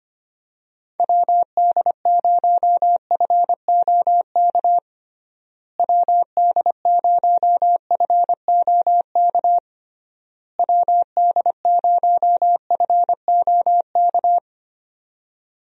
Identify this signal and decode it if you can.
Morse code, heard as a keyed tone, about 25 wpm, 700 Hz. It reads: WD0FOK WD0FOK WD0FOK
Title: USA Call Signs - 6 Characters - Repeated 3x - 25wpm